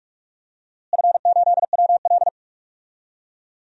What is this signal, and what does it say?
Morse code, heard as a keyed tone, about 46 wpm, 700 Hz. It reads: F8WL
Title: DXCC-100 Call Signs - 4 Characters - Repeated 2x - 45wpm